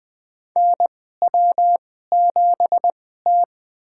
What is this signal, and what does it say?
Morse code, heard as a keyed tone, about 20 wpm, spaced as for 15 wpm, 700 Hz. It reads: NW7T
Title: USA Call Signs - 4 Characters - 2x Character Spacing - 20wpm